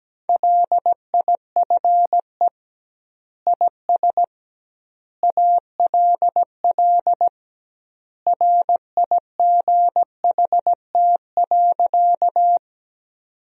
Morse code, heard as a keyed tone, about 17 wpm, 700 Hz. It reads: LIFE IS ALL RIGHT.
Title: 2x Word-Spacing - Sentences from Top 300 Words - 17wpm